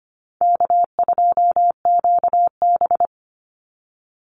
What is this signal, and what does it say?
Morse code, heard as a keyed tone, about 25 wpm, 700 Hz. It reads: K2QB
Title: Call Signs Repeated 2x - 25wpm